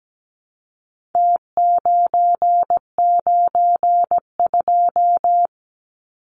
Morse code, heard as a keyed tone, about 17 wpm, 700 Hz. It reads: T992
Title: Numbers - Contest Style - 4-Digit 17wpm